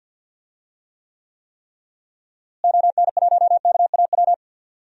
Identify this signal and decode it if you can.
Morse code, heard as a keyed tone, about 51 wpm, 700 Hz. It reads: ON1KAW
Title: DXCC-100 Call Signs - 6 Characters - Repeated 3x - 50wpm